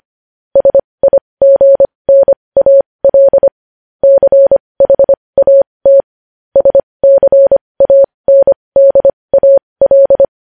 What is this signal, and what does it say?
Morse code, heard as a keyed tone, about 25 wpm, spaced as for 20 wpm, 550 Hz. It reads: SIGNAL CHAT SCANDAL
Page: news-headlines-in-morse-code-at-20-wpm